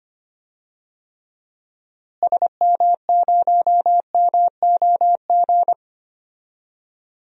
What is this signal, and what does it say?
Morse code, heard as a keyed tone, about 25 wpm, 700 Hz. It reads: SM0MOG
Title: DXCC-100 Call Signs - 6 Characters - Repeated 2x - 25wpm